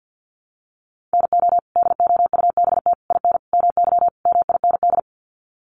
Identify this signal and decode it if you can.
Morse code, heard as a keyed tone, about 50 wpm, 700 Hz. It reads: NO DOUBT IN MY MIND
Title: Sets of 5 Words Encore2 50wpm